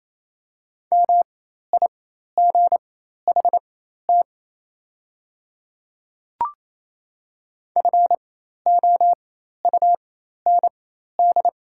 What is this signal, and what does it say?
Morse code, heard as a keyed tone, about 28 wpm, spaced as for 13 wpm, 700 Hz. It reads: MIGHT FOUND
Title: 4x Character Spacing - Top 300 Words - 28wpm